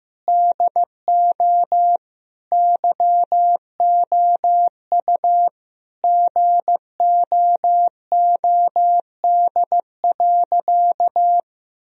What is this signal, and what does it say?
Morse code, heard as a keyed tone, about 15 wpm, 700 Hz. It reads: DO YOU GOOD.
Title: Sentences from Top 1000 Words - Review Part 1 - 15wpm